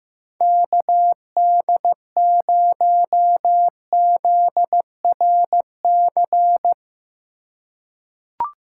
Morse code, heard as a keyed tone, about 15 wpm, 700 Hz. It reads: KD0ZRC
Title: USA Call Signs - 6 Characters - 3x Character Spacing - 15wpm